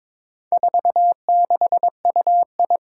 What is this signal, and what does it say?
Morse code, heard as a keyed tone, about 22 wpm, 700 Hz. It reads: 46UI
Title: Four Challenging Letter-Number Combinations 22wpm